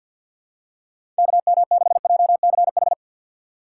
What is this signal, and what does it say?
Morse code, heard as a keyed tone, about 53 wpm, 700 Hz. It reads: KM6JCH
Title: USA Call Signs - 6 Characters - 3x Character Spacing - 50wpm